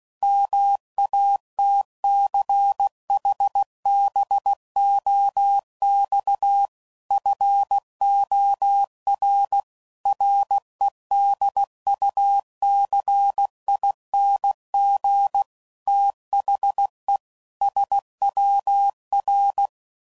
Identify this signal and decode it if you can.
Morse code, heard as a keyed tone, about 16 wpm, 800 Hz. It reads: MATCHBOX FOR REDUCING THE SWR
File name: CW Code Groups @15WPM/QSOs/WD0FBT Spaced